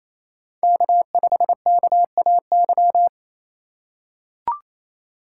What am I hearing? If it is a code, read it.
Morse code, heard as a keyed tone, about 28 wpm, 700 Hz. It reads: K5KAY
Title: USA Call Signs - 5 Characters - 4x Character Spacing - 28wpm